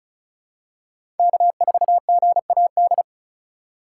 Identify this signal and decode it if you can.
Morse code, heard as a keyed tone, about 36 wpm, 700 Hz. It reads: K4GAD
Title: USA Call Signs - 5 Characters - 6x Character Spacing - 35wpm